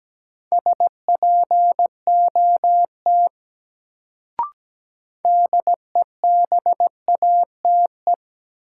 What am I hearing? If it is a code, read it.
Morse code, heard as a keyed tone, about 17 wpm, 700 Hz. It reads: SPOT DEBATE
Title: Sentences from Top 1100 Words - New Word Drill Alt - 17wpm